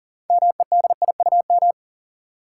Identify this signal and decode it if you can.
Morse code, heard as a keyed tone, about 37 wpm, 700 Hz. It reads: MEDIUM
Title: Six Letter Words - 40wpm